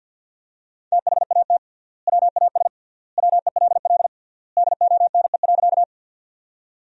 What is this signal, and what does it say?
Morse code, heard as a keyed tone, about 52 wpm, 700 Hz. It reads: THAT WAS WELL DONE.
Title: 3x Word-Spacing - Sentences from Top 500 Words - 50wpm